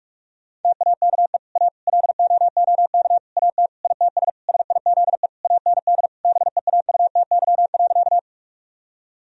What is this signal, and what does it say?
Morse code, heard as a keyed tone, about 41 wpm, 700 Hz. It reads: TAKE A LOOK AT ITS SIZE AND BEAUTY.
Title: Sentences from Top 1000 Words - Review Part 2 - 45wpm